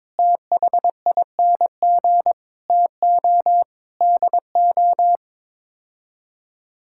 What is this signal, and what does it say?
Morse code, heard as a keyed tone, about 22 wpm, 700 Hz. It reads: THING TO DO
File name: practice / 22wpm / Sets of 6 Words - 22wpm